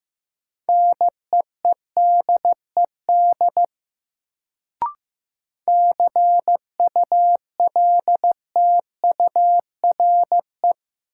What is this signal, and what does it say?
Morse code, heard as a keyed tone, about 15 wpm, 700 Hz. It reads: NEEDED CULTURE
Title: Sentences from Top 1000 Words - Review Part 6 - New Word Drill - 15wpm